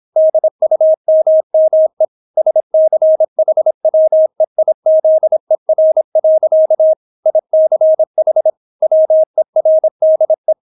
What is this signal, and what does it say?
Morse code, heard as a keyed tone, about 26 wpm, 630 Hz. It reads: DUMME SCHWEIZER. ICH WERDE